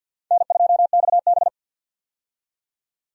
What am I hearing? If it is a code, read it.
Morse code, heard as a keyed tone, about 51 wpm, 700 Hz. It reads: N2XB
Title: USA Call Signs - 4 Characters - 50wpm